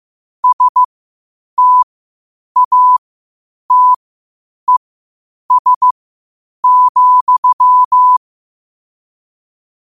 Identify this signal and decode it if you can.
Morse code, heard as a keyed tone, about 15 wpm, spaced as for 8 wpm, 1000 Hz. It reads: STATES,
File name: qotdfiles / 8wpm